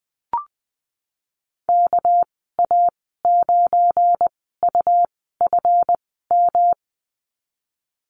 Morse code, heard as a keyed tone, about 20 wpm, spaced as for 15 wpm, 700 Hz. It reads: KA9UFM
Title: 245 Lesson 39g - Letter z - Call Signs 20wpm